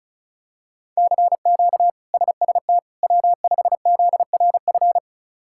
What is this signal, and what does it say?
Morse code, heard as a keyed tone, about 36 wpm, 700 Hz. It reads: CQ SST W5ZRF
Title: K1USN SST QSO Practice - 35wpm